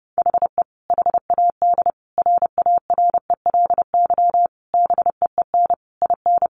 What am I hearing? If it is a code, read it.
Morse code, heard as a keyed tone, about 28 wpm, 700 Hz. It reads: HE HAD RARELY BEEN IN
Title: Sentences from Top 1400 Words 30wpm